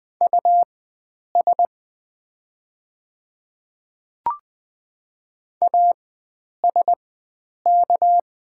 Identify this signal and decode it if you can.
Morse code, heard as a keyed tone, about 20 wpm, spaced as for 9 wpm, 700 Hz. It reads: US ASK
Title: 4x Character Spacing - Top 300 Words - 20wpm